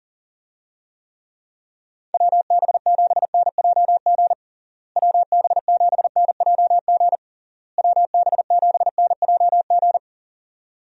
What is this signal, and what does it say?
Morse code, heard as a keyed tone, about 40 wpm, 700 Hz. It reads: WB7NJG WB7NJG WB7NJG
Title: USA Call Signs - 6 Characters - Repeated 3x - 40wpm